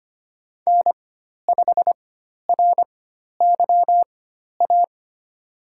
Morse code, heard as a keyed tone, about 25 wpm, spaced as for 12 wpm, 700 Hz. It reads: N5RYA
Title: USA Call Signs - 5 Characters - 4x Character Spacing - 25wpm